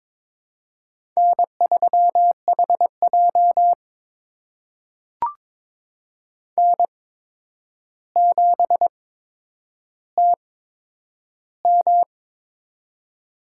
Morse code, heard as a keyed tone, about 22 wpm, spaced as for 9 wpm, 700 Hz. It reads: N3HJ N7TM
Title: USA Call Signs - 4 Characters - 8x Character Spacing - 22wpm